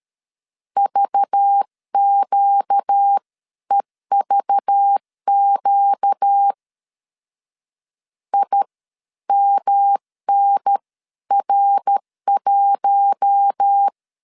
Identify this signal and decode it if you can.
Morse code, heard as a keyed tone, about 13 wpm, spaced as for 11 wpm, 800 Hz. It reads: VQEVQ IMNR1
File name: CW / DET / 1 6 1500C S C 12wpm 3533